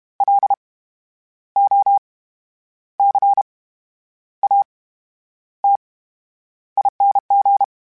Morse code, heard as a keyed tone, about 32 wpm, spaced as for 10 wpm, 800 Hz. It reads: LOCATING
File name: Special INGs / Special INGs-mp3s / VerySlowToFaster Ing @30